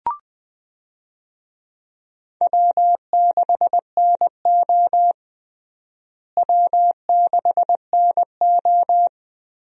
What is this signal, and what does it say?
Morse code, heard as a keyed tone, about 21 wpm, 700 Hz. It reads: W6NO W6NO
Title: USA Call Signs - 4 Characters - Repeated 3x - 20wpm